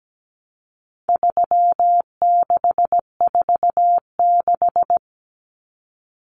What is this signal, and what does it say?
Morse code, heard as a keyed tone, about 17 wpm, 700 Hz. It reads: 3646
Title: Common Cut Numbers - 4-Digit 17wpm